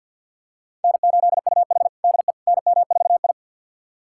Morse code, heard as a keyed tone, about 51 wpm, 700 Hz. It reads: N8UH DE NM4I
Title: USA Callsign DE Callsign - 4 Characters - 50wpm